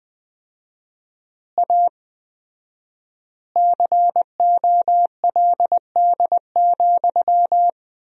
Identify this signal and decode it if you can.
Morse code, heard as a keyed tone, about 20 wpm, 700 Hz. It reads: A COLD,
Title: 4x Word-Spacing - Sentences from Top 500 Words - 20wpm